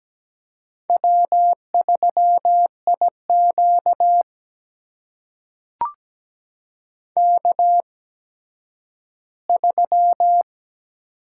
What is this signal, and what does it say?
Morse code, heard as a keyed tone, about 17 wpm, spaced as for 10 wpm, 700 Hz. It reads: W3IQ K3
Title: USA Call Signs - 4 Characters - 8x Character Spacing - 17wpm